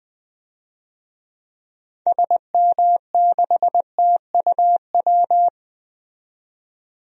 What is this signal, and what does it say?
Morse code, heard as a keyed tone, about 20 wpm, 700 Hz. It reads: SM6TUW
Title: DXCC-100 Call Signs - 6 Characters - 20wpm